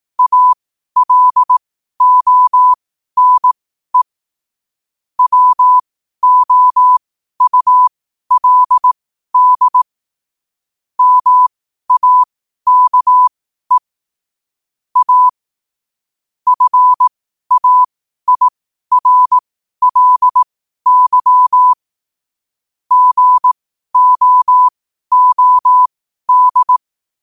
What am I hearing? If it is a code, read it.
Morse code, heard as a keyed tone, about 18 wpm, spaced as for 12 wpm, 1000 Hz. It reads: ALONE WOULD MAKE A FAIRLY GOOD
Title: 12 WPM morse code quotes for Thu, 29 Feb 2024